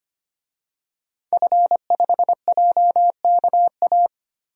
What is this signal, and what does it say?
Morse code, heard as a keyed tone, about 25 wpm, 700 Hz. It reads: F5JKA
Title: DXCC-100 Call Signs - 5 Characters - 25wpm